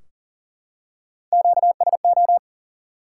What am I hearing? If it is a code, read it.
Morse code, heard as a keyed tone, about 40 wpm, 700 Hz. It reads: QSO
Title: Top 180 QSO Elements 40wpm